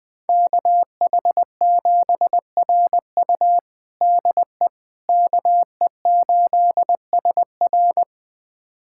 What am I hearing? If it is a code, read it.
Morse code, heard as a keyed tone, about 21 wpm, 700 Hz. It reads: KH7RU DE KE8SR
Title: USA Callsign DE Callsign - 5 Characters - 20wpm